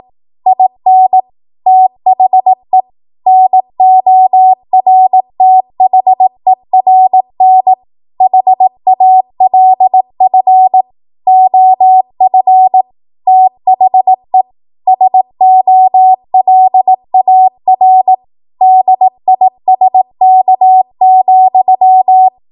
Morse code, heard as a keyed tone, about 18 wpm, 750 Hz. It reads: IN THE NORTHERN HALF OF THE SOLAR DISK,
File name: Morse / 18WPM